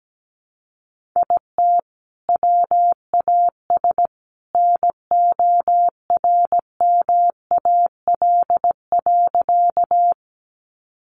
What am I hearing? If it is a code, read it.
Morse code, heard as a keyed tone, about 17 wpm, 700 Hz. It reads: IT WAS NORMAL.
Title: Sentences from Top 1200 Words 17wpm